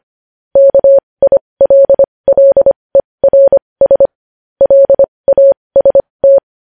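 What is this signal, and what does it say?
Morse code, heard as a keyed tone, about 25 wpm, spaced as for 20 wpm, 550 Hz. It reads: KILLERS LAST
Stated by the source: Morse code news headlines sent at 20 wpm.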